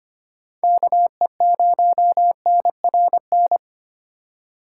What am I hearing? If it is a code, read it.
Morse code, heard as a keyed tone, about 25 wpm, 700 Hz. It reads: KE0NRN
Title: USA Call Signs - 6 Characters - 4x Character Spacing - 25wpm